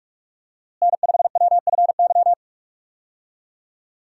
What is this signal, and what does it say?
Morse code, heard as a keyed tone, about 41 wpm, 700 Hz. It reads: N5WFY
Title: USA Call Signs - 5 Characters - Repeated 3x - 45wpm